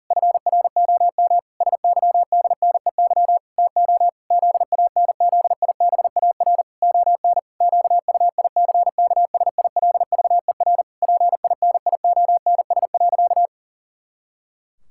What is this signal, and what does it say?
Morse code, heard as a keyed tone, about 40 wpm, 700 Hz. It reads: FROM SYDNEY TO ZANZIBAR ON QUICKSILVER PINIONS.
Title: Pangram Challenge v2 - 40wpm